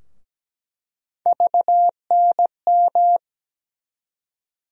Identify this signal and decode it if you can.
Morse code, heard as a keyed tone, about 17 wpm, 700 Hz. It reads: VNM
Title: USA Call Sign Postfixes - 3 Letters 17wpm